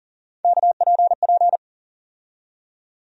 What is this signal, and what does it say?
Morse code, heard as a keyed tone, about 40 wpm, 700 Hz. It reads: KPP
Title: USA Call Sign Postfixes - 3 Letters 40wpm